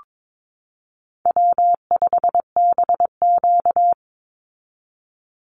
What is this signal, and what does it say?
Morse code, heard as a keyed tone, about 22 wpm, 700 Hz. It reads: W5BQ
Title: Easy Call Signs - 22wpm